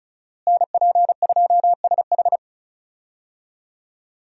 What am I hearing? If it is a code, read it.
Morse code, heard as a keyed tone, about 34 wpm, 700 Hz. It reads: NP2SH
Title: USA Call Signs - 5 Characters - Repeated 2x - 35wpm